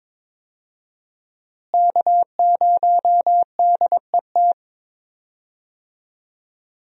Morse code, heard as a keyed tone, about 22 wpm, 700 Hz. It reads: K0DET
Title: USA Call Signs - 5 Characters - Repeated 3x - 22wpm